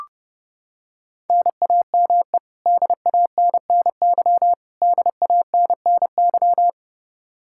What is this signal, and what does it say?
Morse code, heard as a keyed tone, about 30 wpm, 700 Hz. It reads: NAME DANNY DANNY
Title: Practice QSO - Exchange 1 - Beginning - 30wpm